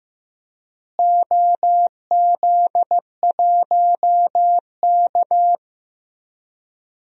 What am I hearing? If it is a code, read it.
Morse code, heard as a keyed tone, about 15 wpm, 700 Hz. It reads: OZ1K
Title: DXCC-100 Call Signs - 4 Characters - Repeated 2x - 15wpm